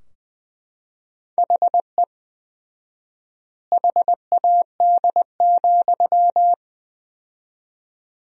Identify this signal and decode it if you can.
Morse code, heard as a keyed tone, about 20 wpm, 700 Hz. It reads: HE HAD,
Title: 4x Word-Spacing - Sentences from Top 500 Words - 20wpm